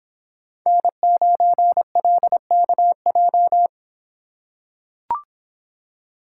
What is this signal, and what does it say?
Morse code, heard as a keyed tone, about 26 wpm, 700 Hz. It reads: N9LKJ